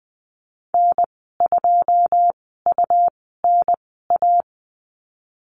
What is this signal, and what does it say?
Morse code, heard as a keyed tone, about 20 wpm, spaced as for 14 wpm, 700 Hz. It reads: N2UNA
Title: 173 Lesson 28g - Number 9 - Call Signs 20wpm